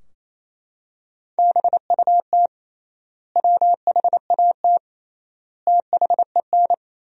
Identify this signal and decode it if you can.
Morse code, heard as a keyed tone, about 28 wpm, 700 Hz. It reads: BUT WHAT THEN
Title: 3x Word-Spacing - Sentences from Top 500 Words - 28wpm